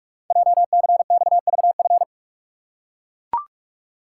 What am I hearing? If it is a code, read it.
Morse code, heard as a keyed tone, about 44 wpm, 700 Hz. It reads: JCXVF
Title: Code Groups - 5 Letters - 45wpm